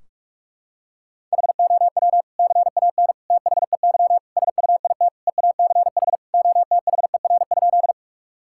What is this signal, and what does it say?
Morse code, heard as a keyed tone, about 46 wpm, 700 Hz. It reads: HOW CAN THEY SUIT EACH OTHER?
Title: Sentences from Top 1600 Words 45wpm